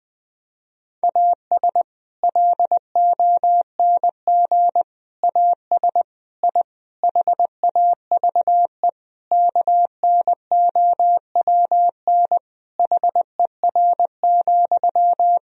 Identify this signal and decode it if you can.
Morse code, heard as a keyed tone, about 20 wpm, 700 Hz. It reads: AS LONG AS I HAVE KNOWN HER,
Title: Sentences from Top 1000 Words - Review Part 3 - 20wpm